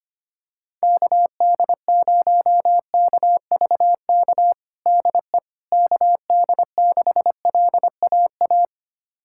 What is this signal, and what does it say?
Morse code, heard as a keyed tone, about 25 wpm, 700 Hz. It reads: KD0KVK DE KD6LAA
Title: USA Callsign DE Callsign - 6 Characters - 25wpm